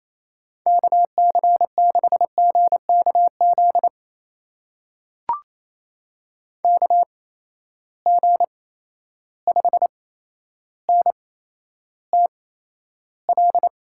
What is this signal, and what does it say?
Morse code, heard as a keyed tone, about 28 wpm, spaced as for 12 wpm, 700 Hz. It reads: KC6GKZ KG5NTL
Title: USA Call Signs - 6 Characters - 8x Character Spacing - 28wpm